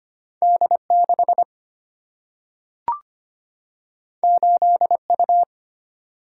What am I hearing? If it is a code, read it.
Morse code, heard as a keyed tone, about 25 wpm, 700 Hz. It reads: D 6 8 U
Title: Single-Letter Single-Number 25wpm